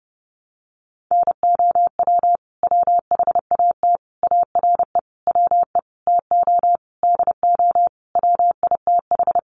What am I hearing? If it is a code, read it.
Morse code, heard as a keyed tone, about 28 wpm, 700 Hz. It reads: NOW WHAT ARE WE TO DO WITH
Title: Sentences from Top 100 Words 30wpm